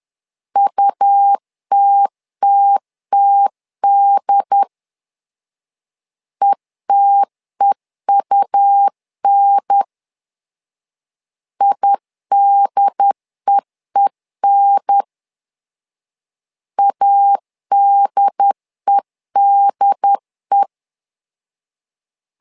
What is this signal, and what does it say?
Morse code, heard as a keyed tone, about 10 wpm, 800 Hz. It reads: UTTTD ETEUN IDEEN ADEDE
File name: LEZIONI / 2 UHDN / 2 UHDN 10wpm 1904